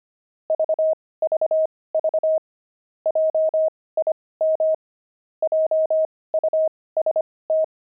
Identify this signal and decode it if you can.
Morse code, heard as a keyed tone, about 25 wpm, spaced as for 18 wpm, 630 Hz. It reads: VVV JIM JUST
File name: pangram-fs-18-cs-25-quit